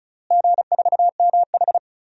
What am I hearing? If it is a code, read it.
Morse code, heard as a keyed tone, about 36 wpm, 700 Hz. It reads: G4MH
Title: DXCC-100 Call Signs - 4 Characters - Repeated 2x - 35wpm